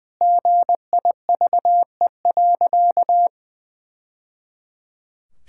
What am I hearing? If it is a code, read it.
Morse code, heard as a keyed tone, about 21 wpm, 700 Hz. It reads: GIVE.
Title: Sentences from Top 1000 Words - Review Part 6 - 20wpm